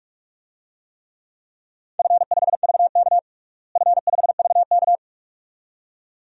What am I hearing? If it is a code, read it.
Morse code, heard as a keyed tone, about 43 wpm, 700 Hz. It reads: F5VK F5VK
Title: DXCC-100 Call Signs - 4 Characters - Repeated 2x - 45wpm